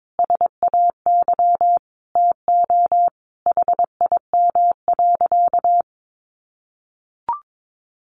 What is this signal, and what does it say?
Morse code, heard as a keyed tone, about 22 wpm, 700 Hz. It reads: SAY TO HIM.
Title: Sentences from Top 1000 Words 22wpm